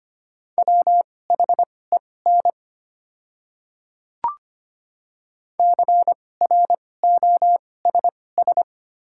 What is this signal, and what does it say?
Morse code, heard as a keyed tone, about 25 wpm, spaced as for 18 wpm, 700 Hz. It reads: WHEN CROSS
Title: 2x Character Spacing - Top 500 Words - 25wpm